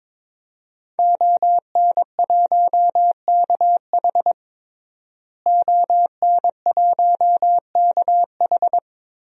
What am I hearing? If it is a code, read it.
Morse code, heard as a keyed tone, about 22 wpm, 700 Hz. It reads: ON1KH ON1KH
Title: DXCC-100 Call Signs - 5 Characters - Repeated 3x - 22wpm